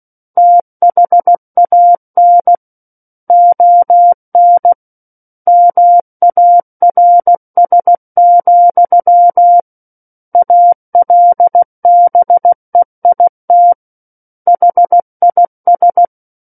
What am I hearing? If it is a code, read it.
Morse code, heard as a keyed tone, about 16 wpm, 700 Hz. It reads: THAN ON MARS, ALBEIT HIS
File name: Book One / 07-Chapter 7 - 16 WPM